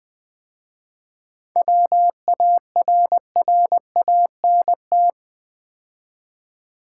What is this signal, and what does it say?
Morse code, heard as a keyed tone, about 20 wpm, 700 Hz. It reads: WARRANT
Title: 7 Letter Words - First 3 Letters Given - 20wpm